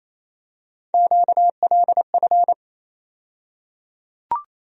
Morse code, heard as a keyed tone, about 28 wpm, 700 Hz. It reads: QLF
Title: USA Call Sign Postfixes - 3 Letters 28wpm